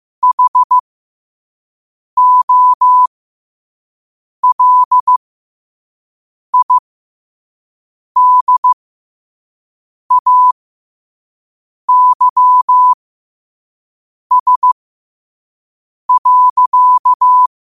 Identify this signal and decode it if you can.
Morse code, heard as a keyed tone, about 15 wpm, spaced as for 5 wpm, 1000 Hz. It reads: HOLIDAYS.
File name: qotdfiles / 5wpm